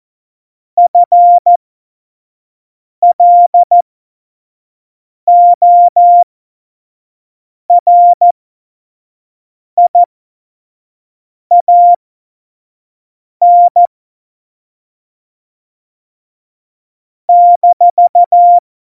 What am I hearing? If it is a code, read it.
Morse code, heard as a keyed tone, about 14 wpm, spaced as for 5 wpm, 700 Hz. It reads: FLORIAN -
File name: Morsekurs / Witze / Witz 09. April 2019, 25 F70